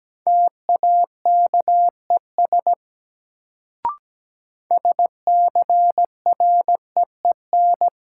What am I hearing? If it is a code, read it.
Morse code, heard as a keyed tone, about 17 wpm, 700 Hz. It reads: TAKES SCREEN
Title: Sentences from Top 1000 Words - Review Part 6 - New Word Drill - 17wpm